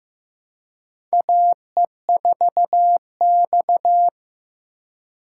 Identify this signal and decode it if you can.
Morse code, heard as a keyed tone, about 15 wpm, 700 Hz. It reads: AE4X
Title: USA Call Signs - 4 Characters - 3x Character Spacing - 15wpm